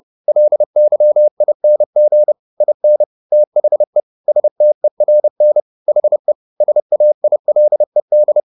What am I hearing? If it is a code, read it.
Morse code, heard as a keyed tone, about 30 wpm, 600 Hz. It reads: LYING IN THE STERN HE SAILED